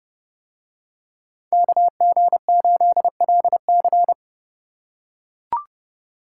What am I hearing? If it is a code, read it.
Morse code, heard as a keyed tone, about 30 wpm, 700 Hz. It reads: KG8LC